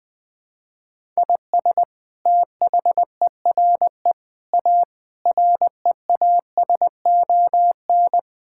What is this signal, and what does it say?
Morse code, heard as a keyed tone, about 20 wpm, 700 Hz. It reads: IS THERE A REASON
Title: Sentences from Top 1000 Words - Review Part 5 - 20wpm